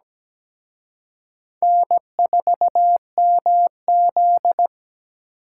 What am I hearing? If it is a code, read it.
Morse code, heard as a keyed tone, about 20 wpm, 700 Hz. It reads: N4MZ
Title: Speed Racing - USA 1x2 Call Signs 17wpm